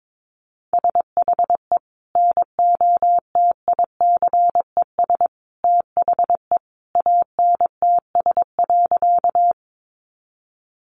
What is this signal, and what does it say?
Morse code, heard as a keyed tone, about 22 wpm, 700 Hz. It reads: SHE NOTICES THE ANTS.
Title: Additional Easy Phrases 22wpm